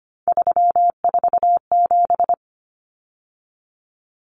Morse code, heard as a keyed tone, about 25 wpm, 700 Hz. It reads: 347
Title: Numbers - 3-Digit 25wpm